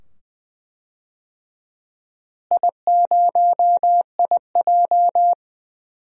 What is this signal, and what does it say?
Morse code, heard as a keyed tone, about 20 wpm, 700 Hz. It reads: I0IJ
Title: DXCC-100 Call Signs - 4 Characters - Repeated 2x - 20wpm